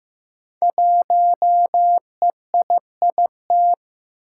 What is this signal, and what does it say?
Morse code, heard as a keyed tone, about 15 wpm, 700 Hz. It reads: 1EIIT
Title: Five Challenging Letter-Number Combinations 15wpm